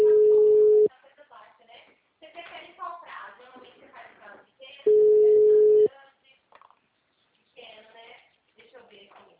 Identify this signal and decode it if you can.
Ringback tone (Europe, 425 Hz), with speech.
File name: out- -2001- - - .25642